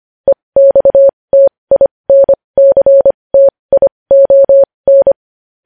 Morse code, heard as a keyed tone, about 25 wpm, spaced as for 20 wpm, 550 Hz. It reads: EXTINCTION
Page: news headlines in morse code at 20 wpm